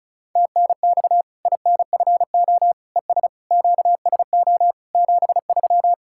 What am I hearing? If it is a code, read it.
Morse code, heard as a keyed tone, about 34 wpm, 700 Hz. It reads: TNX INFO ES QSO 73
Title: Practice QSO - Exchange 4 - Ending - 35wpm